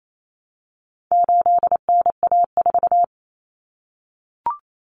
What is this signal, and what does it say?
Morse code, heard as a keyed tone, about 28 wpm, 700 Hz. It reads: 8NA4
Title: Common Cut Numbers - 4-Digit 28wpm